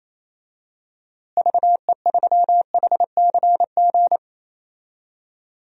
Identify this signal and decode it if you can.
Morse code, heard as a keyed tone, about 28 wpm, 700 Hz. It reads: VE3HCG
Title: DXCC-100 Call Signs - 6 Characters - Repeated 3x - 28wpm